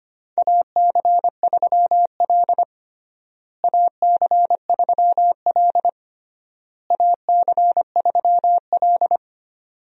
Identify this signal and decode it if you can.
Morse code, heard as a keyed tone, about 25 wpm, 700 Hz. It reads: AC3L AC3L AC3L
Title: USA Call Signs - 4 Characters - Repeated 3x - 25wpm